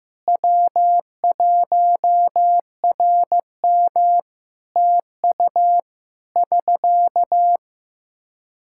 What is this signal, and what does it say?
Morse code, heard as a keyed tone, about 16 wpm, 700 Hz. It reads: W1RM TU <SK>
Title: Practice QSO - Exchange 4 - Ending - 15wpm